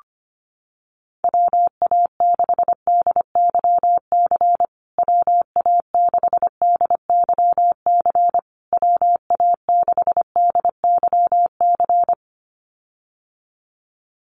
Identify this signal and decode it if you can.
Morse code, heard as a keyed tone, about 25 wpm, 700 Hz. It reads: WA6DYC WA6DYC WA6DYC
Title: USA Call Signs Repeated 3x 25wpm